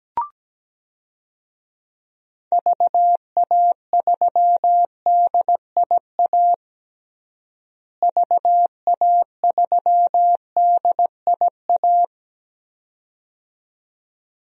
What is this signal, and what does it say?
Morse code, heard as a keyed tone, about 17 wpm, 700 Hz. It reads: VA3DIA VA3DIA
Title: DXCC-100 Call Signs - 6 Characters - Repeated 2x - 17wpm